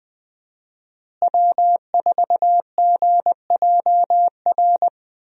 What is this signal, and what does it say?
Morse code, heard as a keyed tone, about 20 wpm, 700 Hz. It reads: W4GJR
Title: USA Call Signs - 5 Characters - Repeated 3x - 20wpm